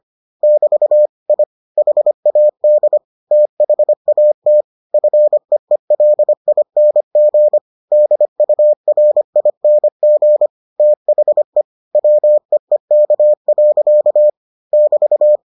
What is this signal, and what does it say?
Morse code, heard as a keyed tone, about 25 wpm, 600 Hz. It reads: <BT> I HAD THAT FEELING DURING THE WEEK. <BT>